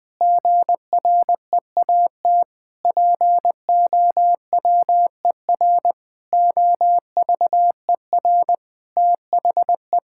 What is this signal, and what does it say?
Morse code, heard as a keyed tone, about 20 wpm, 700 Hz. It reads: GREAT POWER OVER THE
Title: Sentences from Top 1000 Words - Review Part 5 - 20wpm